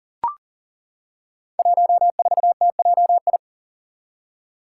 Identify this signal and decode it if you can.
Morse code, heard as a keyed tone, about 38 wpm, 700 Hz. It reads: 14TJI
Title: Five Challenging Letter-Number Combinations 40wpm